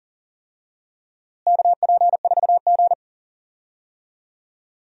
Morse code, heard as a keyed tone, about 39 wpm, 700 Hz. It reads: KP4G
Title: DXCC-100 Call Signs - 4 Characters - Repeated 2x - 40wpm